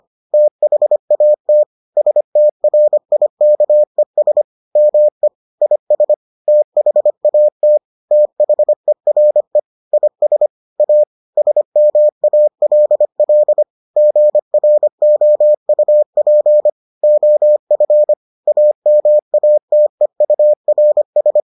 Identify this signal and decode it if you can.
Morse code, heard as a keyed tone, about 25 wpm, 600 Hz. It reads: THAT STRIKES ME IS THAT THERE IS A SMALL GROUP OF AMATEURS